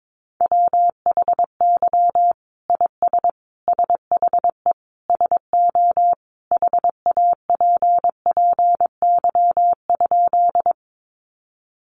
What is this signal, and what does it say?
Morse code, heard as a keyed tone, about 22 wpm, 700 Hz. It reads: WHY IS SHE SO HAPPY?
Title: Sentences from Top 1000 Words 22wpm